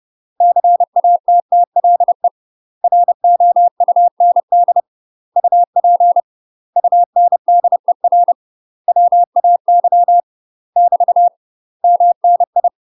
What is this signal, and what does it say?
Morse code, heard as a keyed tone, about 30 wpm, 700 Hz. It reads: CATTLE ROUND UP UNDER WAY <BT> MNI
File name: UK MASTER QSOs TRAINING / 30WPM / UK MASTER QSOs 0022